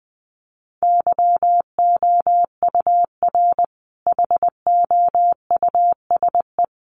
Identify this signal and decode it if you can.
Morse code, heard as a keyed tone, about 20 wpm, 700 Hz. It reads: YOUR HOUSE
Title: Sentences from Top 1300 Words 20wpm